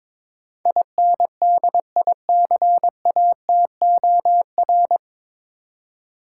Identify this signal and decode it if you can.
Morse code, heard as a keyed tone, about 21 wpm, 700 Hz. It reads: INDICATOR
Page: Morse Code Ninja practice 22wpm